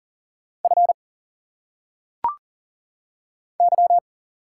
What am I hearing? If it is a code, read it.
Morse code, heard as a keyed tone, about 40 wpm, 700 Hz. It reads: F Y
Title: Single Letters - Phonetic Pronunciation 40wpm